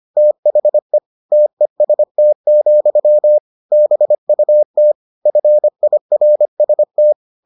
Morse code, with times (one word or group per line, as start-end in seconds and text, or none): 0.2-1.0 THE
1.3-3.4 TEST,
3.7-4.9 BUT
5.3-7.1 FIRST